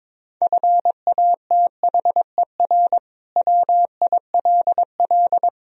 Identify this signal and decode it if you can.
Morse code, heard as a keyed tone, about 21 wpm, 700 Hz. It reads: FATHER WILL
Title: Sentences from Top 1000 Words - Review Part 3 - 22wpm